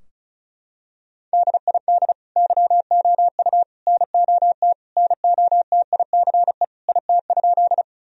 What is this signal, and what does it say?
Morse code, heard as a keyed tone, about 36 wpm, 700 Hz. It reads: DID YOU NOT NOTICE IT?
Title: Sentences from Top 1000 Words - Review Part 2 - 35wpm